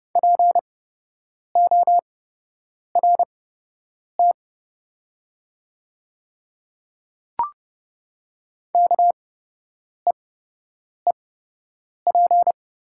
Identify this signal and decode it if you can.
Morse code, heard as a keyed tone, about 30 wpm, spaced as for 8 wpm, 700 Hz. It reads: PORT KEEP
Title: 8x Character Spacing - Top 500 Words - 30wpm